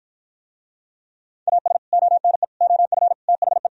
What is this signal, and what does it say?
Morse code, heard as a keyed tone, about 52 wpm, 700 Hz. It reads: AS ONE OF THE